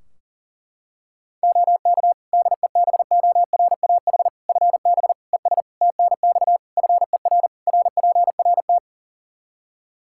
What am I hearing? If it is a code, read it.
Morse code, heard as a keyed tone, about 40 wpm, 700 Hz. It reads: OK DEBORAH FB ES TNX FER RPRT
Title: Practice QSO - Exchange 2 - Info - 40wpm